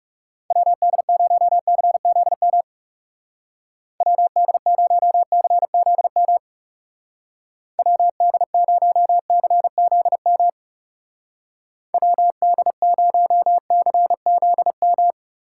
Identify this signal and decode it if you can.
Morse code, heard as a keyed tone, about 36 wpm, 700 Hz. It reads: WD0CZM WD0CZM WD0CZM WD0CZM
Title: Speed Racing - USA 2x3 Call Signs 30wpm